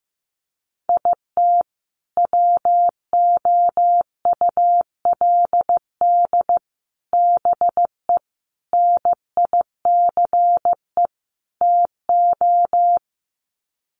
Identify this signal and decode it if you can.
Morse code, heard as a keyed tone, about 15 wpm, 700 Hz. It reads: IT WOULD BE NICE TO
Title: Sets of 5 Words Encore2 15wpm